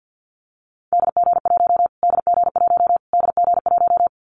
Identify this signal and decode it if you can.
Morse code, heard as a keyed tone, about 50 wpm, 700 Hz. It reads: DG1 DG1 DG1
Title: Callsign Prefixes for Top 25 DXCC - Repeated 3x - 50wpm